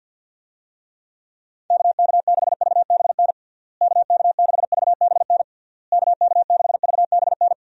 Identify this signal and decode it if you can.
Morse code, heard as a keyed tone, about 53 wpm, 700 Hz. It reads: KK6VBN KK6VBN KK6VBN
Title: USA Call Signs - 6 Characters - Repeated 3x - 50wpm